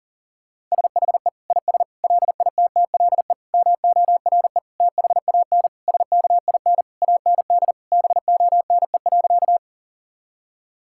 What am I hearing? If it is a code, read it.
Morse code, heard as a keyed tone, about 40 wpm, 700 Hz. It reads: SHE IS LITTLE MORE THAN SKIN AND BONE.
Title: Sentences from Top 1600 Words 40wpm